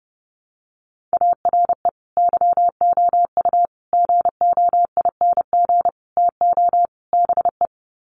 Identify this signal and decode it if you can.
Morse code, heard as a keyed tone, about 30 wpm, 700 Hz. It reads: ARE YOU GOING TO BE
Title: Sets of 5 Words Encore 30wpm